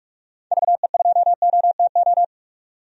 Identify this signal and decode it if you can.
Morse code, heard as a keyed tone, about 42 wpm, 700 Hz. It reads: VE2OTO
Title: DXCC-100 Call Signs - 6 Characters - 45wpm